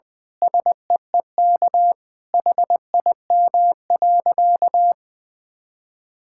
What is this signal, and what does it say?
Morse code, heard as a keyed tone, about 21 wpm, 700 Hz. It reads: SEEK HIM.
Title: Sentences from Top 1600 Words 20wpm